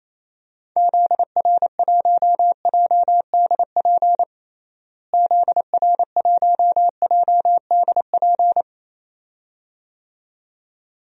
Morse code, heard as a keyed tone, about 28 wpm, 700 Hz. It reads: ZR1JDP ZR1JDP
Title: DXCC-100 Call Signs - 6 Characters - Repeated 2x - 28wpm